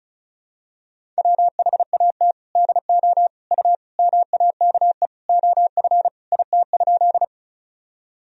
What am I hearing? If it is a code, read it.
Morse code, heard as a keyed tone, about 34 wpm, 700 Hz. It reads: WHAT DO U MAKE OF IT?
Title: Sentences from Top 100 Words and Abbreviations 35wpm